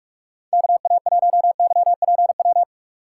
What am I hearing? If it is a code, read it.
Morse code, heard as a keyed tone, about 46 wpm, 700 Hz. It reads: KA1YPW